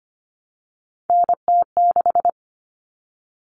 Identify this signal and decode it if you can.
Morse code, heard as a keyed tone, about 25 wpm, 700 Hz. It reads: NT6
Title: All Cut Numbers - 3-Digit 25wpm